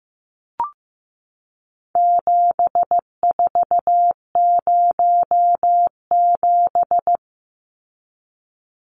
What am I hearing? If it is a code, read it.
Morse code, heard as a keyed tone, about 14 wpm, 700 Hz. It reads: 7407
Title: Numbers - 4-Digit 15wpm